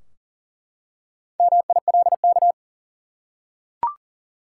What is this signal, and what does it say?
Morse code, heard as a keyed tone, about 40 wpm, 700 Hz. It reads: MILK
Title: Four Letter Words - 40wpm